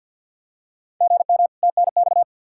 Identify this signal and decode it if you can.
Morse code, heard as a keyed tone, about 50 wpm, 700 Hz. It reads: GM TNX
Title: Practice QSO - Exchange 1 - Beginning - 50wpm